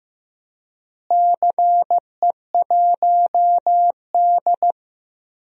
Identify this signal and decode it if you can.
Morse code, heard as a keyed tone, about 15 wpm, 700 Hz. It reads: CE1D
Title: DXCC-100 Call Signs - 4 Characters - Repeated 2x - 15wpm